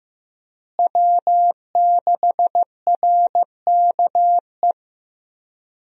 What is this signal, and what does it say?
Morse code, heard as a keyed tone, about 15 wpm, 700 Hz. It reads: W6RKE
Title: USA Call Signs - 5 Characters - Repeated 3x - 15wpm